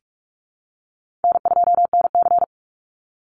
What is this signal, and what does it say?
Morse code, heard as a keyed tone, about 44 wpm, 700 Hz. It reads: N2NC
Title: Call Signs Repeated 3x - 45wpm